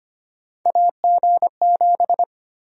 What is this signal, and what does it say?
Morse code, heard as a keyed tone, about 25 wpm, 700 Hz. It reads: AG7
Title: USA Call Sign Prefixes - 25wpm